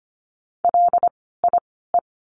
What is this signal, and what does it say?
Morse code, heard as a keyed tone, about 25 wpm, spaced as for 16 wpm, 700 Hz. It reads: LIE
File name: course / 25:10wpm / 094 Lesson 16f - Letter c - Words 25wpm